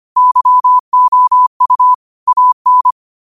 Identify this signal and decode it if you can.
Morse code, heard as a keyed tone, about 25 wpm, 1000 Hz. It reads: YOU AN
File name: qotdfiles / 25wpm